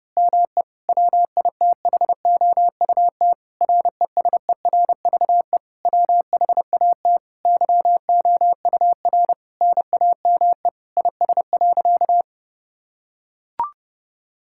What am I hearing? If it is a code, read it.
Morse code, heard as a keyed tone, about 29 wpm, 700 Hz. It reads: ME WITHOUT RESERVE WHAT YOUR NAME IS.
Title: Sentences from Top 1000 Words - Review Part 6 - 30wpm